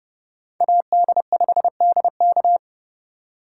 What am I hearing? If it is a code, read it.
Morse code, heard as a keyed tone, about 30 wpm, 700 Hz. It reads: AD5DK